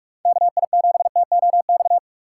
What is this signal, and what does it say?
Morse code, heard as a keyed tone, about 45 wpm, 700 Hz. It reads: KI7TOX